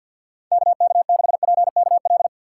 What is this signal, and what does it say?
Morse code, heard as a keyed tone, about 51 wpm, 700 Hz. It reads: KK6PKL
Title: USA Call Signs - 6 Characters - 6x Character Spacing - 50wpm